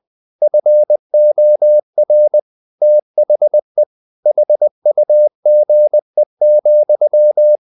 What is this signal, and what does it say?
Morse code, heard as a keyed tone, about 20 wpm, 600 Hz. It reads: FOR THE HUGE,
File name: the old man and the sea / the old man and the sea part 1 20wpm